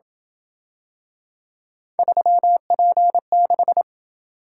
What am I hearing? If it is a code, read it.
Morse code, heard as a keyed tone, about 27 wpm, 700 Hz. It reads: 3P6